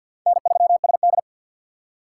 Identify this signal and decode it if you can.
Morse code, heard as a keyed tone, about 52 wpm, 700 Hz. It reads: N3SD
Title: USA Call Signs - 2x Character Spacing - 50wpm